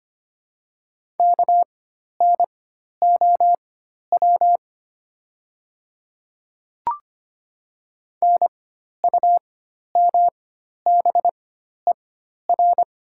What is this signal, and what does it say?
Morse code, heard as a keyed tone, about 25 wpm, spaced as for 12 wpm, 700 Hz. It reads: KNOW NUMBER
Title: 4x Character Spacing - Top 100 Words - 25wpm